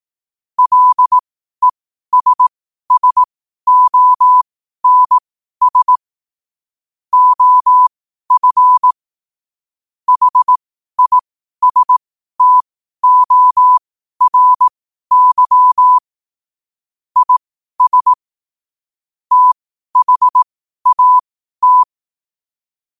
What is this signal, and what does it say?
Morse code, heard as a keyed tone, about 18 wpm, spaced as for 12 wpm, 1000 Hz. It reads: LESSONS OF HISTORY IS THAT
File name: qotdfiles / 12wpm